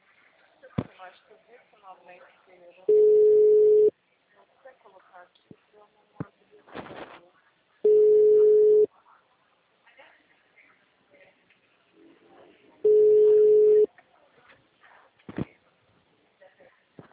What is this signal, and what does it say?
Ringback tone (Europe, 425 Hz), with speech.
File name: out- -2008- - - .5249